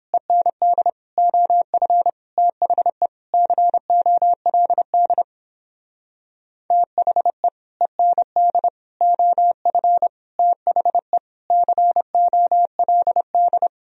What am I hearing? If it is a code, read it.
Morse code, heard as a keyed tone, about 29 wpm, 700 Hz. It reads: END OF THE COLD THE END OF THE COLD